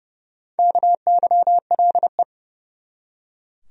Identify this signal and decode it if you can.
Morse code, heard as a keyed tone, about 32 wpm, 700 Hz. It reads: KYLE
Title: USA Common Names - Max 4 Letters - 30wpm